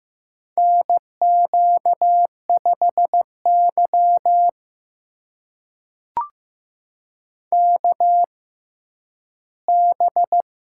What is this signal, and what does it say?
Morse code, heard as a keyed tone, about 15 wpm, spaced as for 10 wpm, 700 Hz. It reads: NQ5Y KB
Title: USA Call Signs - 4 Characters - 6x Character Spacing - 15wpm